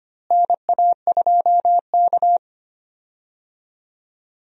Morse code, heard as a keyed tone, about 25 wpm, 700 Hz. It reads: NA2K
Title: USA Call Signs - 4 Characters - Repeated 2x - 25wpm